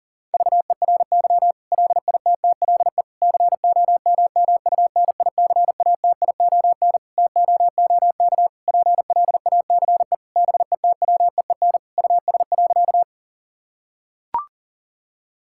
Morse code, heard as a keyed tone, about 40 wpm, 700 Hz. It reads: VERY LITTLE COMMUNICATION TOOK PLACE BETWEEN US.
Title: Sentences from Top 1000 Words - Review Part 6 - 40wpm